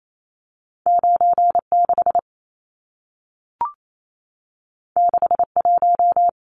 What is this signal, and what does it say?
Morse code, heard as a keyed tone, about 28 wpm, 700 Hz. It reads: 96 61
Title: Numbers - 2-Digit 28wpm